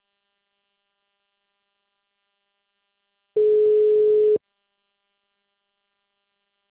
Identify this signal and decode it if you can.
Ringback tone (Europe, 425 Hz).
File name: out- -2013- - - .38052